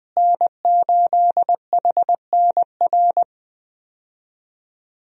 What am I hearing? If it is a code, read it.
Morse code, heard as a keyed tone, about 20 wpm, 700 Hz. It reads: N8HNR
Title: USA Call Signs - 5 Characters - Repeated 3x - 20wpm